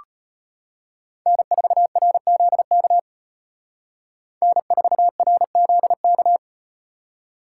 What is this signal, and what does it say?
Morse code, heard as a keyed tone, about 38 wpm, 700 Hz. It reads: N4RZK N4RZK